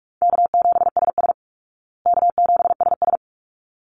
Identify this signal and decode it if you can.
Morse code, heard as a keyed tone, about 44 wpm, 700 Hz. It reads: K7SS K7SS
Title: Easy Call Signs Repeated 2x - 45wpm